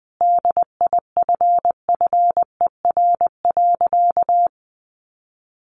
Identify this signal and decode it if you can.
Morse code, heard as a keyed tone, about 20 wpm, 700 Hz. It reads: DIFFER.
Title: Sentences from Top 1100 Words 20wpm